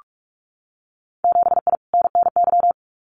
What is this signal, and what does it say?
Morse code, heard as a keyed tone, about 46 wpm, 700 Hz. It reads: 7I NNY
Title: Winter Field Day Exchanges Encore 45wpm